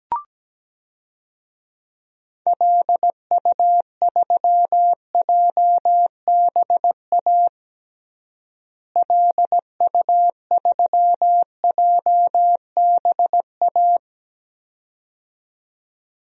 Morse code, heard as a keyed tone, about 17 wpm, 700 Hz. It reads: LU3JBA LU3JBA
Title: DXCC-100 Call Signs - 6 Characters - Repeated 2x - 17wpm